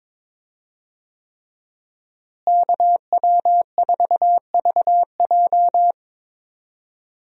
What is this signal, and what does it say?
Morse code, heard as a keyed tone, about 22 wpm, 700 Hz. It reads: KW4VJ
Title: USA Call Signs - 5 Characters - Repeated 3x - 22wpm